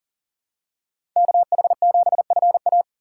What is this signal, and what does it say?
Morse code, heard as a keyed tone, about 40 wpm, 700 Hz. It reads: KH7FA
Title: USA Call Signs - 5 Characters - 4x Character Spacing - 40wpm